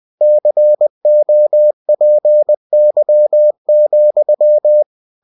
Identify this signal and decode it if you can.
Morse code, heard as a keyed tone, about 20 wpm, 600 Hz. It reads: COPY,
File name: cw-books / winnie-the-pooh / winnie-20wpm-page-0117